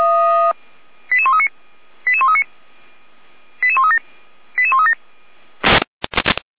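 Five-tone selective call (ZVEI), heard as quick runs of five tones, tones of about 70 ms. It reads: 88139 88139 88137 88137
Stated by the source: ZVEI five-tone selective call received with an SDR.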